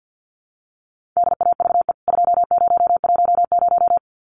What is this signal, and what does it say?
Morse code, heard as a keyed tone, about 50 wpm, 700 Hz. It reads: DAVE 2010
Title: CWT Exchange 50wpm